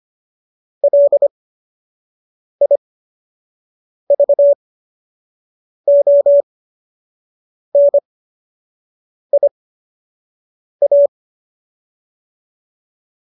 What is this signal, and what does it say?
Morse code, heard as a keyed tone, about 25 wpm, spaced as for 6 wpm, 570 Hz. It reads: LIVONIA